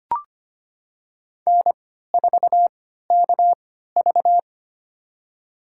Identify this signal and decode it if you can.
Morse code, heard as a keyed tone, about 25 wpm, spaced as for 14 wpm, 700 Hz. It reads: N4KV
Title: USA Call Signs - 3x Character Spacing - 25wpm